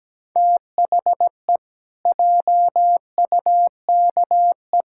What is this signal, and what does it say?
Morse code, heard as a keyed tone, about 17 wpm, 700 Hz. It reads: THE JUKE
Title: Pangram Challenge v2 - 17wpm